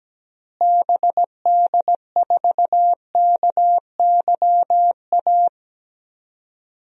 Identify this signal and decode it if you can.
Morse code, heard as a keyed tone, about 17 wpm, 700 Hz. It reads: BD4KYA
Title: DXCC-100 Call Signs - 6 Characters - Repeated 2x - 17wpm